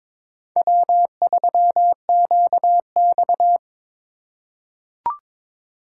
Morse code, heard as a keyed tone, about 21 wpm, 700 Hz. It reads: W3QX
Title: USA Call Signs - 4x Character Spacing - 22wpm